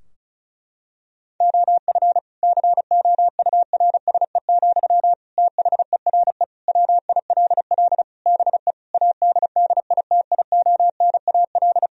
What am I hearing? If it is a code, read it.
Morse code, heard as a keyed tone, about 34 wpm, 700 Hz. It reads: OF COURSE, THERE WILL BE ADDITIONAL
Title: Sentences from Top 1000 Words - Review Part 4 - 35wpm